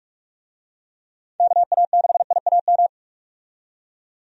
Morse code, heard as a keyed tone, about 46 wpm, 700 Hz. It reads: KA6IAM
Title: USA Call Signs - 6 Characters - 45wpm